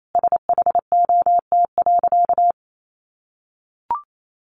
Morse code, heard as a keyed tone, about 28 wpm, 700 Hz. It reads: SHOT.
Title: Sentences from Top 1100 Words 28wpm